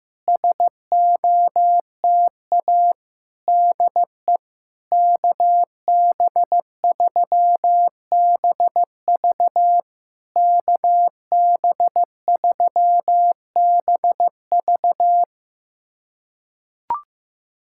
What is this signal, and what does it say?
Morse code, heard as a keyed tone, about 15 wpm, 700 Hz. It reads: SOTA DE KB3BV KB3BV
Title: SOTA QSO Practice 15wpm